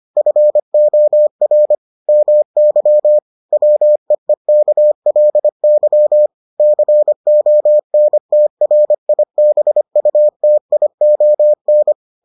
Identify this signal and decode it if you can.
Morse code, heard as a keyed tone, about 25 wpm, 600 Hz. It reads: FOR MY WEEKLY CONTRIBUTION